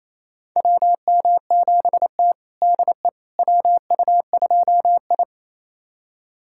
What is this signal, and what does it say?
Morse code, heard as a keyed tone, about 28 wpm, 700 Hz. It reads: WM7T DE WU2I
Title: USA Callsign DE Callsign - 4 Characters - 28wpm